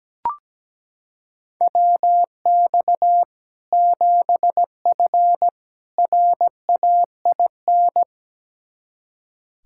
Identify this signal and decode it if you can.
Morse code, heard as a keyed tone, about 17 wpm, 700 Hz. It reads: WX 7F RAIN
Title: Practice QSO - Exchange 2 - Info - 17wpm